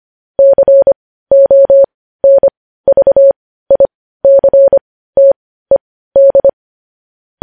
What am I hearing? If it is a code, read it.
Morse code, heard as a keyed tone, about 25 wpm, spaced as for 15 wpm, 550 Hz. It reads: CONVICTED
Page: news-headlines-in-morse-code-at-15-wpm